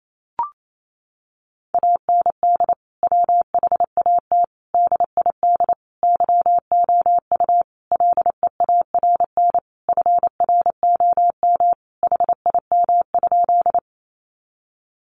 Morse code, heard as a keyed tone, about 28 wpm, 700 Hz. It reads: AND WHAT DID YOU LEARN FROM HIM?
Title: Sentences from Top 300 Words 28wpm